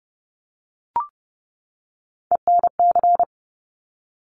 Morse code, heard as a keyed tone, about 30 wpm, 690 Hz. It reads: ENC
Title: Top Three Letter Combinations in Words 30wpm